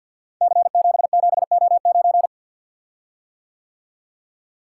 Morse code, heard as a keyed tone, about 47 wpm, 700 Hz. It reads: C77O9
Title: Code Groups - 5 Letter:Numbers - 50wpm